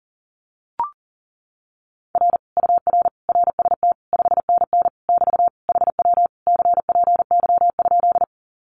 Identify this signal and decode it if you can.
Morse code, heard as a keyed tone, about 40 wpm, 700 Hz. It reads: R UR RST 5NN <BT> HW CPY?
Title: Common QSO Phrases Encore 40wpm